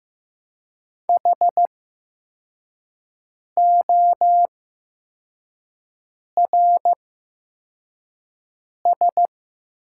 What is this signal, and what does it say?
Morse code, heard as a keyed tone, about 16 wpm, spaced as for 4 wpm, 700 Hz. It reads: HORS
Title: 8x Character Spacing - Top 300 Words - 15wpm